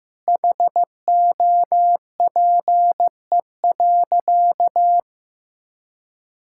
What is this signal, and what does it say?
Morse code, heard as a keyed tone, about 15 wpm, 700 Hz. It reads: HOPE.
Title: Sentences from Top 1000 Words - Review Part 5 - 15wpm